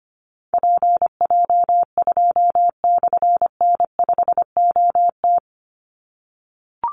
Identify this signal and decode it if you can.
Morse code, heard as a keyed tone, about 25 wpm, 700 Hz. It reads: PJ2/N5OT
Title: Call Signs - 25wpm